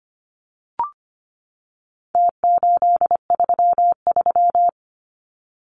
Morse code, heard as a keyed tone, about 25 wpm, 700 Hz. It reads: T833
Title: Numbers - Contest Style - 4-Digit 25wpm